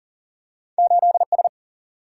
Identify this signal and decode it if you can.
Morse code, heard as a keyed tone, about 39 wpm, 700 Hz. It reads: 8S
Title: Single-Letter Single-Number 40wpm